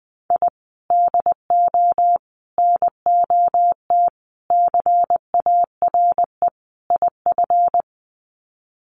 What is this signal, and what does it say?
Morse code, heard as a keyed tone, about 20 wpm, 700 Hz. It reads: I DO NOT CARE IF
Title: Sets of 5 Words 20wpm